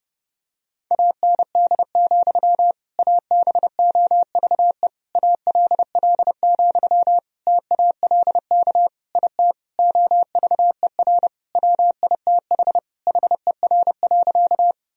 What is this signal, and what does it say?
Morse code, heard as a keyed tone, about 30 wpm, 700 Hz. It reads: AND, ABOVE ALL, TALK IT OVER WITH HER.
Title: Sentences from Top 1000 Words - Review Part 2 - 30wpm